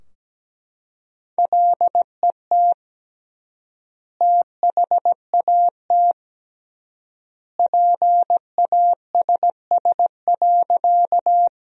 Morse code, heard as a keyed tone, about 17 wpm, 700 Hz. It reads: LET THAT PASS.
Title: 3x Word-Spacing - Sentences from Top 400 Words - 17wpm